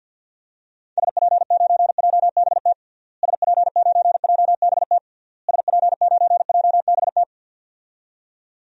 Morse code, heard as a keyed tone, about 47 wpm, 700 Hz. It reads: SP9JBT SP9JBT SP9JBT
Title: DXCC-100 Call Signs - 6 Characters - Repeated 3x - 50wpm